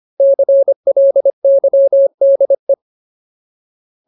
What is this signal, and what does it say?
Morse code, heard as a keyed tone, about 25 wpm, 550 Hz. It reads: CLYDE
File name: Ultimate Word List / 25WPM Ultimate